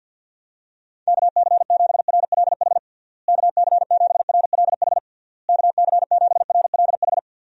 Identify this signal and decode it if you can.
Morse code, heard as a keyed tone, about 46 wpm, 700 Hz. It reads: KC7RLH KC7RLH KC7RLH
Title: USA Call Signs - 6 Characters - Repeated 3x - 50wpm